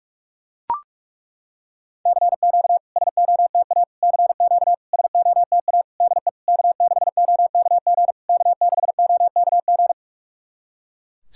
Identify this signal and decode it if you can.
Morse code, heard as a keyed tone, about 42 wpm, 690 Hz. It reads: CQ SOTA CQ SOTA DE K6OKG K6OKG
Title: SOTA QSO Practice 45wpm